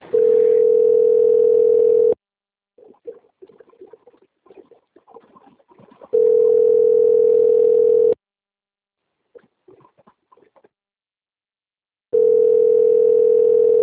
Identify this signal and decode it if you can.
Ringback tone (North America), 440 + 480 Hz.
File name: out- - - - - .49168